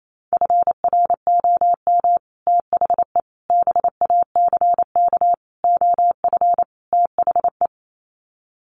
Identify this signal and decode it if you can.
Morse code, heard as a keyed tone, about 28 wpm, 700 Hz. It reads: FROM THE BACK OF THE
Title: Sets of 5 Words Encore 28wpm